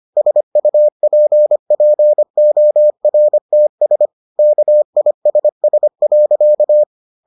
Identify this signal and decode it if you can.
Morse code, heard as a keyed tone, about 25 wpm, 600 Hz. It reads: SUPPORTS KISS.